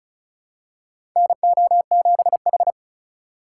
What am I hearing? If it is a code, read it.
Morse code, heard as a keyed tone, about 34 wpm, 690 Hz. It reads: NO7H
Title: USA Call Signs - 4 Characters - 2x Character Spacing - 35wpm